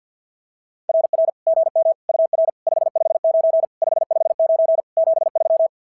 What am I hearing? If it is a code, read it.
Morse code, heard as a keyed tone, about 53 wpm, 590 Hz. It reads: RR GM UR 559 559 73
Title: SOTA QSO Practice 50wpm